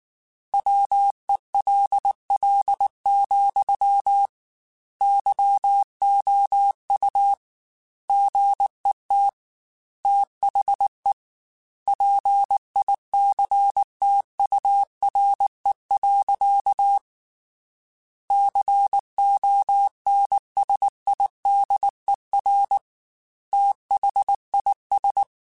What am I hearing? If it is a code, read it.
Morse code, heard as a keyed tone, about 19 wpm, 800 Hz. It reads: WELL, YOU GET THE PICTURE. CONSIDER THIS
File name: Technical Subjects / Technical Subjects-mp3s / AND Gate @18